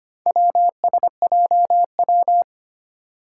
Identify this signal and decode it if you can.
Morse code, heard as a keyed tone, about 25 wpm, 700 Hz. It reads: WSJW
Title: Four Challenging Letter-Number Combinations 25wpm